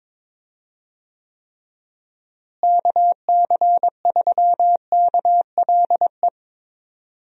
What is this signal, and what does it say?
Morse code, heard as a keyed tone, about 22 wpm, 700 Hz. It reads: KC3KLE
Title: USA Call Signs - 6 Characters - Repeated 3x - 22wpm